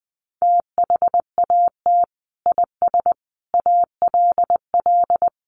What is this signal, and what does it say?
Morse code, heard as a keyed tone, about 20 wpm, 700 Hz. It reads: THAT IS ALL
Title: Sentences from Top 200 Words 20wpm